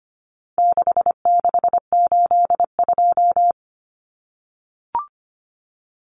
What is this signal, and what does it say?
Morse code, heard as a keyed tone, about 25 wpm, 700 Hz. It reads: 6682
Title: Numbers - 4-Digit 25wpm